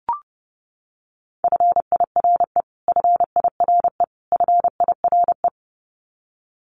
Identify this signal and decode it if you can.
Morse code, heard as a keyed tone, about 30 wpm, 700 Hz. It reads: FIRE FIRE FIRE
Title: Top 500 Words Repeated 30wpm